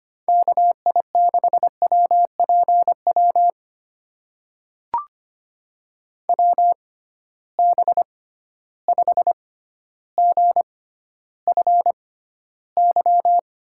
Morse code, heard as a keyed tone, about 25 wpm, spaced as for 13 wpm, 700 Hz. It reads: KI6WPW WB5GFY
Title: USA Call Signs - 6 Characters - 6x Character Spacing - 25wpm